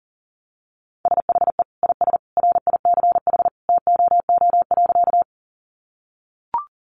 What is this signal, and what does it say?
Morse code, heard as a keyed tone, about 40 wpm, 700 Hz. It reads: SHE IS RICH TOO.
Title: Sentences from Top 1100 Words 40wpm